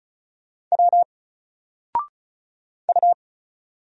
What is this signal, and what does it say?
Morse code, heard as a keyed tone, about 35 wpm, 700 Hz. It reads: W U
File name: Single-Letter-Number-35wpm